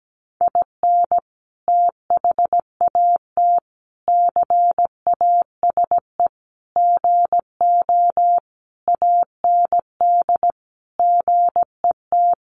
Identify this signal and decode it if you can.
Morse code, heard as a keyed tone, about 17 wpm, 700 Hz. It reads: IN THAT CASE GO AND GET
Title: Sentences from Top 700 Words 17wpm